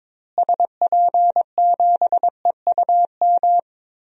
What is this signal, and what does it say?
Morse code, heard as a keyed tone, about 22 wpm, 700 Hz. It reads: SP7EUM
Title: DXCC-100 Call Signs - 6 Characters - Repeated 2x - 22wpm